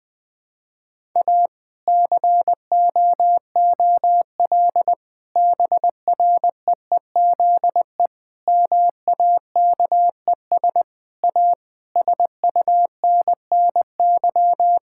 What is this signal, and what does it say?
Morse code, heard as a keyed tone, about 20 wpm, 700 Hz. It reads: A COOL BREEZE MAKES A SUNNY
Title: Difficult Sentences with Hints - 20wpm